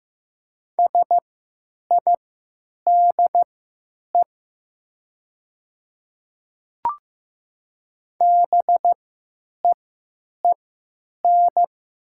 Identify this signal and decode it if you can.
Morse code, heard as a keyed tone, about 15 wpm, spaced as for 9 wpm, 700 Hz. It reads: SIDE BEEN
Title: 3x Character Spacing - Top 500 Words - 15wpm